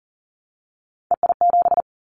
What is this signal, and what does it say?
Morse code, heard as a keyed tone, about 40 wpm, 700 Hz. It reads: EI7
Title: Callsign Prefixes for Top 50 DXCC - 40wpm